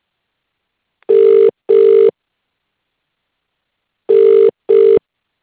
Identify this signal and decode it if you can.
Ringback tone (UK), 400 + 450 Hz.